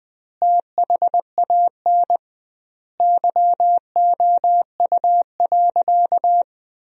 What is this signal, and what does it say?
Morse code, heard as a keyed tone, about 20 wpm, 700 Hz. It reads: THAN YOU.
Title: 2x Word-Spacing - Sentences from Top 400 Words - 20wpm